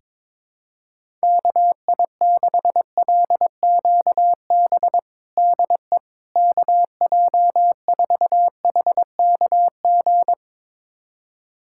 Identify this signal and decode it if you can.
Morse code, heard as a keyed tone, about 22 wpm, 700 Hz. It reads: KI6LQB DE KJ4HKG
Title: USA Callsign DE Callsign - 6 Characters - 22wpm